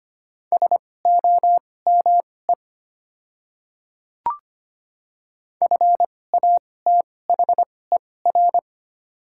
Morse code, heard as a keyed tone, about 25 wpm, spaced as for 18 wpm, 700 Hz. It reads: SOME FATHER
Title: 2x Character Spacing - Top 200 Words - 25wpm